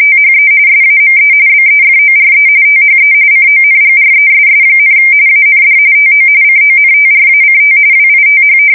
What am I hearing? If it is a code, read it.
RTTY, about 45.45 baud. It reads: THE SAME BEFORE THE CONGRESS, AND TO CONTRACT WITH
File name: fsk170 / usnavy1